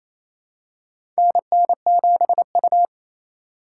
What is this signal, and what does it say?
Morse code, heard as a keyed tone, about 28 wpm, 700 Hz. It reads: NN7U
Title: USA Call Signs - 4 Characters - 2x Character Spacing - 28wpm